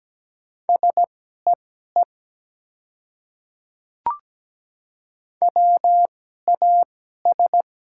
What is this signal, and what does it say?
Morse code, heard as a keyed tone, about 17 wpm, spaced as for 12 wpm, 700 Hz. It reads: SEE WAS
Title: 2x Character Spacing - Top 100 Words - 17wpm